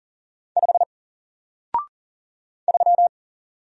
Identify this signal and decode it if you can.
Morse code, heard as a keyed tone, about 40 wpm, 700 Hz. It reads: 5 3
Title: Single Letter-Number Alt 40wpm